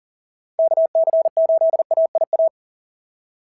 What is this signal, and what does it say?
Morse code, heard as a keyed tone, about 40 wpm, 690 Hz. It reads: KC8AIA
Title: SOTA QSO Practice 40wpm